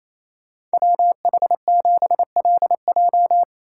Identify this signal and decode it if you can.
Morse code, heard as a keyed tone, about 34 wpm, 700 Hz. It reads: WH7LJ
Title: Speed Racing - USA 2x2 Call Signs 28wpm